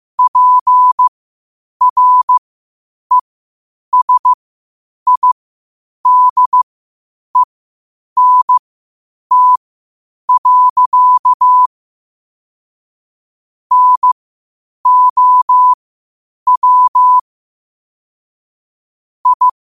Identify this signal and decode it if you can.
Morse code, heard as a keyed tone, about 15 wpm, spaced as for 8 wpm, 1000 Hz. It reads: PRESIDENT. NOW I
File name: qotdfiles / 8wpm